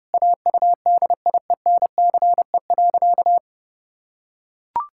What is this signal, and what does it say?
Morse code, heard as a keyed tone, about 32 wpm, 700 Hz. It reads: AUDIENCE.
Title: Sentences from Top 1000 Words - Review Part 5 - 30wpm